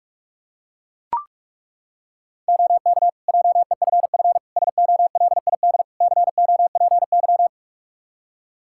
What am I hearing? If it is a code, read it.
Morse code, heard as a keyed tone, about 43 wpm, 700 Hz. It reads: OK JEFF SOLID COPY
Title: K1USN SST QSO Practice 45wpm